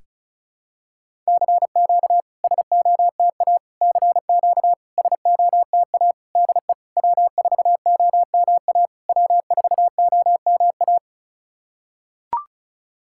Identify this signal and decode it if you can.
Morse code, heard as a keyed tone, about 34 wpm, 690 Hz. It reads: CQ SOTA CQ SOTA DE W4OMA W4OMA
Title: SOTA QSO Practice 35wpm